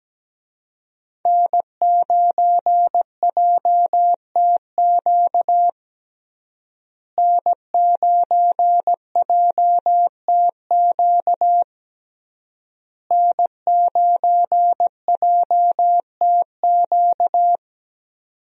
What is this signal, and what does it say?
Morse code, heard as a keyed tone, about 17 wpm, 700 Hz. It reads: N9JTQ N9JTQ N9JTQ
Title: USA Call Signs - 5 Characters - Repeated 3x - 17wpm